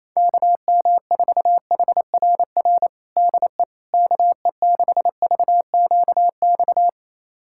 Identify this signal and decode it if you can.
Morse code, heard as a keyed tone, about 28 wpm, 700 Hz. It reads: KM4HRR DE KE6VQX
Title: USA Callsign DE Callsign 28wpm